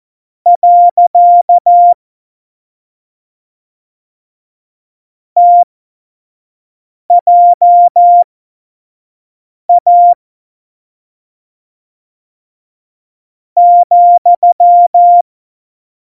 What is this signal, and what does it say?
Morse code, heard as a keyed tone, about 14 wpm, spaced as for 5 wpm, 700 Hz. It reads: . TJA ,
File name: Morsekurs / Witze / Witz 09. Januar 2018, 25 F70